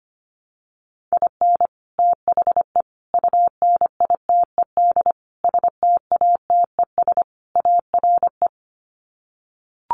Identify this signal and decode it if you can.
Morse code, heard as a keyed tone, about 25 wpm, 700 Hz. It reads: IN THE UNITED STATES ARE
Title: Sets of 5 Words Encore 25wpm